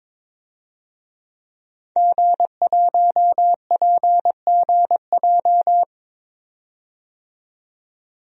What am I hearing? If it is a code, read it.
Morse code, heard as a keyed tone, about 22 wpm, 700 Hz. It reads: G1PGJ
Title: DXCC-100 Call Signs - 5 Characters - 22wpm